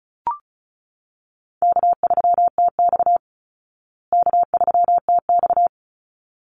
Morse code, heard as a keyed tone, about 34 wpm, 700 Hz. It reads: K3TX K3TX
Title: Easy Call Signs Repeated 2x - 35wpm